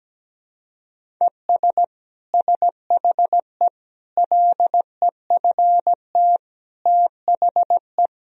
Morse code, heard as a keyed tone, about 17 wpm, 700 Hz. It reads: ES SHE LEFT THE
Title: Sentences from Top 300 Words and Abbreviations 17wpm